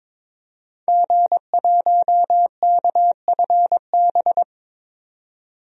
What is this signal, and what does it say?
Morse code, heard as a keyed tone, about 22 wpm, 700 Hz. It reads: G1KFB
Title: DXCC-100 Call Signs - 5 Characters - 22wpm